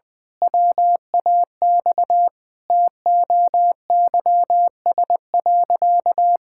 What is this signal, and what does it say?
Morse code, heard as a keyed tone, about 20 wpm, 700 Hz. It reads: WAX TOYS.
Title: Pangram Challenge v2 - 20wpm